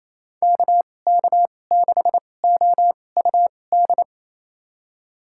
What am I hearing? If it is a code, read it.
Morse code, heard as a keyed tone, about 28 wpm, spaced as for 20 wpm, 700 Hz. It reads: KK6OUD
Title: USA Call Signs - 6 Characters - 2x Character Spacing - 28wpm